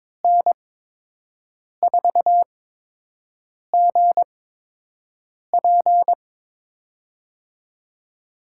Morse code, heard as a keyed tone, about 23 wpm, spaced as for 6 wpm, 700 Hz. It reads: N4GP
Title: USA Call Signs - 8x Character Spacing - 22wpm